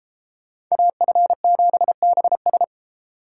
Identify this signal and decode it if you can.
Morse code, heard as a keyed tone, about 35 wpm, 700 Hz. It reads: AF7BS